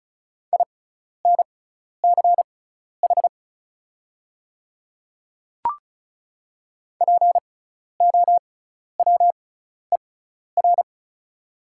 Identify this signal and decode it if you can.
Morse code, heard as a keyed tone, about 36 wpm, spaced as for 12 wpm, 690 Hz. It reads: INCH POWER
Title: 6x Character Spacing - Top 500 Words - 35wpm